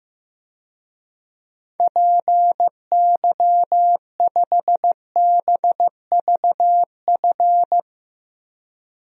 Morse code, heard as a keyed tone, about 16 wpm, 700 Hz. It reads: PY5BVF
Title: DXCC-100 Call Signs - 6 Characters - Repeated 3x - 15wpm